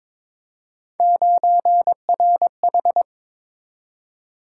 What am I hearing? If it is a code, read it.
Morse code, heard as a keyed tone, about 22 wpm, 700 Hz. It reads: 9RH
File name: practice / 22wpm / Three Letter-Number Combinations - 22wpm